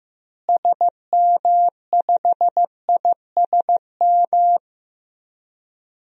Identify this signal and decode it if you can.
Morse code, heard as a keyed tone, about 15 wpm, 700 Hz. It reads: SM5ISM
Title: DXCC-100 Call Signs - 6 Characters - Repeated 3x - 15wpm